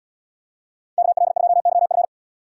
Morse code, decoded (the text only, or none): KL2YF